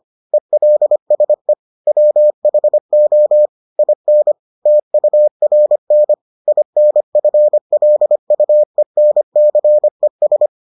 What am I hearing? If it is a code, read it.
Morse code, heard as a keyed tone, about 25 wpm, 600 Hz. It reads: ELSE WHO IN TURN INFLUENCES